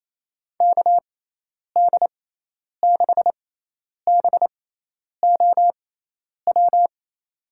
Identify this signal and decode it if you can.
Morse code, heard as a keyed tone, about 28 wpm, spaced as for 10 wpm, 700 Hz. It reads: KD6BOW
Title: USA Call Signs - 6 Characters - 6x Character Spacing - 28wpm